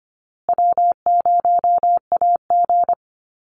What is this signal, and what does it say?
Morse code, heard as a keyed tone, about 25 wpm, 700 Hz. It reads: W0AG
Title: Easy Call Signs Repeated 3x - 25wpm